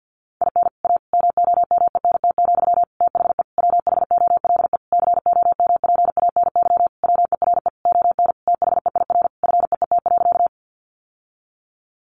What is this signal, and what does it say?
Morse code, heard as a keyed tone, about 50 wpm, 700 Hz. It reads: IN A MOMENT, THE WHOLE COMPANY WERE ON THEIR FEET.
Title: Sentences from Top 1000 Words 50wpm